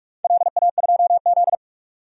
Morse code, heard as a keyed tone, about 43 wpm, 700 Hz. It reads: LA2Z
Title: DXCC-100 Call Signs - 4 Characters - 45wpm